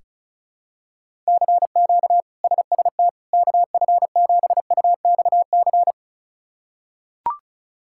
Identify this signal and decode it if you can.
Morse code, heard as a keyed tone, about 36 wpm, 700 Hz. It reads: CQ SST KF7UXC
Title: K1USN SST QSO Practice - 35wpm